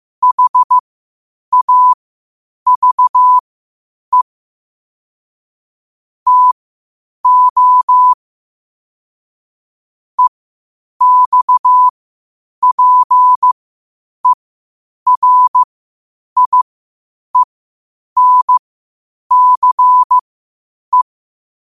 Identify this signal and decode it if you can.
Morse code, heard as a keyed tone, about 15 wpm, spaced as for 8 wpm, 1000 Hz. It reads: HAVE TO EXPERIENCE
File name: qotdfiles / 8wpm